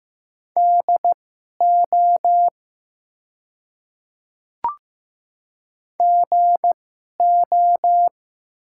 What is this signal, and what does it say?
Morse code, heard as a keyed tone, about 15 wpm, spaced as for 11 wpm, 700 Hz. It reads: DO GO
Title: 2x Character Spacing - Top 100 Words - 15wpm